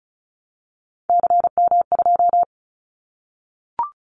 Morse code, heard as a keyed tone, about 34 wpm, 700 Hz. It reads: CM2
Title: Callsign Prefixes for Top 100 DXCC - 35wpm